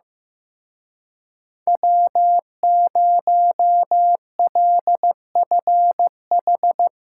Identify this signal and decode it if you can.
Morse code, heard as a keyed tone, about 16 wpm, 700 Hz. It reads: W0LFH
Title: Speed Racing - USA Call Signs 15wpm